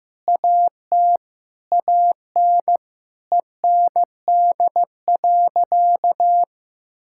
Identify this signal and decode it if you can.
Morse code, heard as a keyed tone, about 15 wpm, 700 Hz. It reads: AT AN END.
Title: Sentences from Top 1600 Words 15wpm